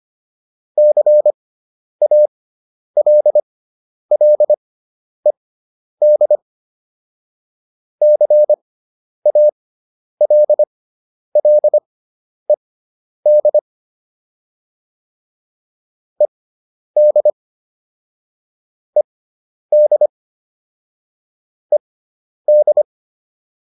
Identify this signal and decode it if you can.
Morse code, heard as a keyed tone, about 25 wpm, spaced as for 10 wpm, 600 Hz. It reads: CALLED CALLED ED ED ED